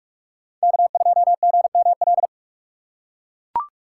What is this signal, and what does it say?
Morse code, heard as a keyed tone, about 44 wpm, 700 Hz. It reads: K2GML